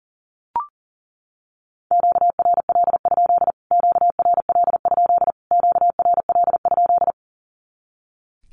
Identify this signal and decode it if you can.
Morse code, heard as a keyed tone, about 40 wpm, 700 Hz. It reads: QRL? QRL? QRL?
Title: Q-Codes Repeated 3x 40wpm